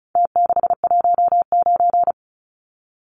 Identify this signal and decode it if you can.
Morse code, heard as a keyed tone, about 34 wpm, 700 Hz. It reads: T619
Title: Numbers - Contest Style - 4-Digit 35wpm